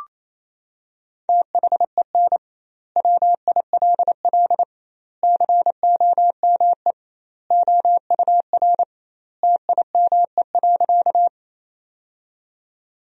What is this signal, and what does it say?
Morse code, heard as a keyed tone, about 28 wpm, 700 Hz. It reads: THEN WILL COME OUR TIME.
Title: 2x Word-Spacing - Sentences from Top 200 Words - 28wpm